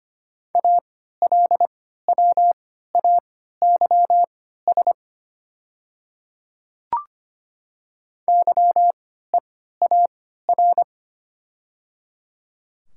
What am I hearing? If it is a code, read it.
Morse code, heard as a keyed tone, about 25 wpm, spaced as for 14 wpm, 700 Hz. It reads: ALWAYS YEAR
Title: 3x Character Spacing - Top 300 Words - 25wpm